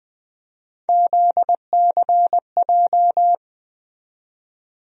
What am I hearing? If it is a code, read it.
Morse code, heard as a keyed tone, about 21 wpm, 700 Hz. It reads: ZCJ
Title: USA Call Sign Postfixes - 3 Letters 20wpm